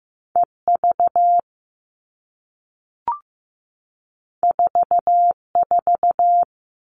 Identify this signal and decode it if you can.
Morse code, heard as a keyed tone, about 15 wpm, 700 Hz. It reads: EV 44
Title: All Cut Numbers - 2-Digit 15wpm